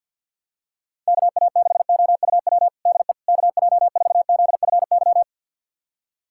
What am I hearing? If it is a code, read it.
Morse code, heard as a keyed tone, about 52 wpm, 700 Hz. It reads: KA6OUW DE KJ4ZFY
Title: USA Callsign DE Callsign - 6 Characters - 50wpm